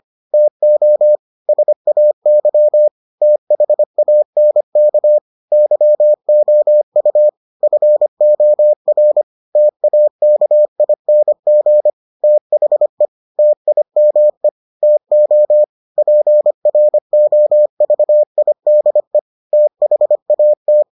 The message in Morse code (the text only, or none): TO SAY THANK YOU FOR TAKING THE TIME TO PROVIDE THAT